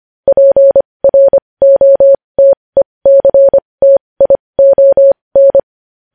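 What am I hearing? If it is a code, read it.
Morse code, heard as a keyed tone, about 25 wpm, spaced as for 20 wpm, 550 Hz. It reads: PROTECTION
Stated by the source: Morse code news headlines sent at 20 wpm.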